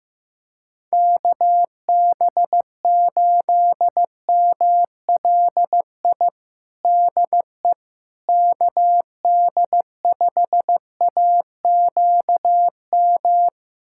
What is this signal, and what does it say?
Morse code, heard as a keyed tone, about 15 wpm, 700 Hz. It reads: KB8MLI DE KD5AQM
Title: USA Callsign DE Callsign 15wpm